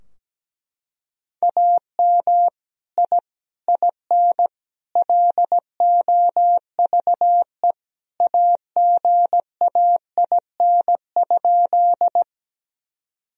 Morse code, heard as a keyed tone, about 17 wpm, 700 Hz. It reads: AM I IN LOVE AGAIN?
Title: Sentences from Top 1000 Words - Review Part 2 - 17wpm